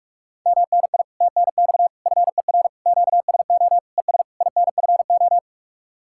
Morse code, heard as a keyed tone, about 48 wpm, 700 Hz. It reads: MNI TNX FER QSO ES INFO